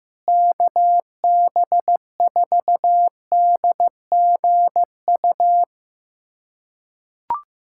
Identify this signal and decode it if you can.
Morse code, heard as a keyed tone, about 16 wpm, 700 Hz. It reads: KB4DGU
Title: USA Callsign DE Callsign - 6 Characters - 15wpm